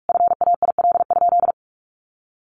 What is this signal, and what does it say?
Morse code, heard as a keyed tone, about 44 wpm, 700 Hz. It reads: FAIL?
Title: Sentences from Top 1400 Words 45wpm